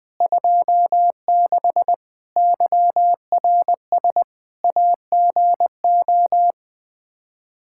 Morse code, heard as a keyed tone, about 21 wpm, 700 Hz. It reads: 26 YRS AGO
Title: Practice QSO - Exchange 3 - More Info - 20wpm